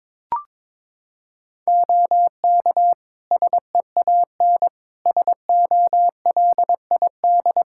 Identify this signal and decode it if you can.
Morse code, heard as a keyed tone, about 22 wpm, 700 Hz. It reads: OK SEAN SOLID
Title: Practice QSO - Exchange 3 - More Info - 22wpm